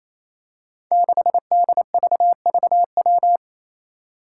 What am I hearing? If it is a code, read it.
Morse code, heard as a keyed tone, about 27 wpm, 700 Hz. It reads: 6DVVW
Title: Five Challenging Letter-Number Combinations 28wpm